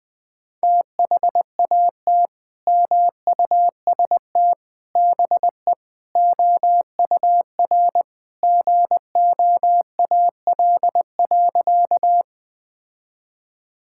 Morse code, heard as a keyed tone, about 21 wpm, 700 Hz. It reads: THAT MUST BE OUR GOAL.
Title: Sentences from Top 1000 Words - Review Part 5 - 20wpm